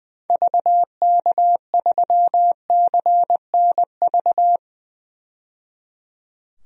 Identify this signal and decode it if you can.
Morse code, heard as a keyed tone, about 20 wpm, 700 Hz. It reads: VK3CNV
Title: DXCC-100 Call Signs - 6 Characters - 20wpm